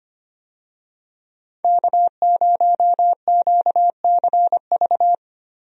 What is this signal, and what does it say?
Morse code, heard as a keyed tone, about 25 wpm, 690 Hz. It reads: K0QCV
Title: USA Call Signs - 5 Characters - 25wpm